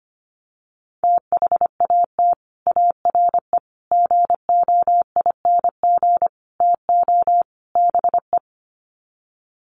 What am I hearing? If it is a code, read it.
Morse code, heard as a keyed tone, about 25 wpm, 700 Hz. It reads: THAT ARE GOING TO BE
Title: Sets of 5 Words Encore2 25wpm